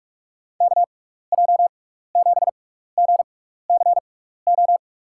Morse code, heard as a keyed tone, about 46 wpm, spaced as for 15 wpm, 700 Hz. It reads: KJ7GCO
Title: USA Call Signs - 6x Character Spacing - 45wpm